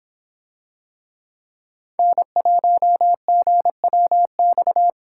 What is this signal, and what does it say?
Morse code, heard as a keyed tone, about 26 wpm, 700 Hz. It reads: N1GWX